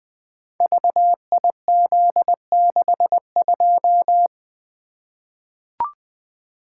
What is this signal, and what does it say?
Morse code, heard as a keyed tone, about 20 wpm, 700 Hz. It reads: VIZ62
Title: Code Groups - 5 Letter:Numbers - 20wpm